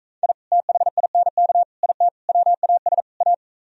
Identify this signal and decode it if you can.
Morse code, heard as a keyed tone, about 43 wpm, 700 Hz. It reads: I THINK IT WAS A